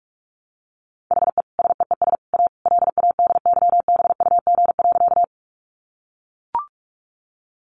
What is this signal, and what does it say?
Morse code, heard as a keyed tone, about 44 wpm, 700 Hz. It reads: HE SEES A LADYBUG.
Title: Additional Easy Phrases 45wpm